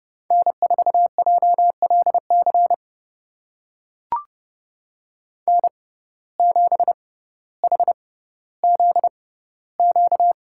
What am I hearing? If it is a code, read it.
Morse code, heard as a keyed tone, about 30 wpm, spaced as for 15 wpm, 700 Hz. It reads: N4JLC N7HZQ
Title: USA Call Signs - 5 Characters - 6x Character Spacing - 30wpm